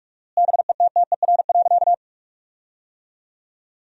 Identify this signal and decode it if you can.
Morse code, heard as a keyed tone, about 46 wpm, 700 Hz. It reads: BETTER.
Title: 4x Word-Spacing - Sentences from Top 500 Words - 45wpm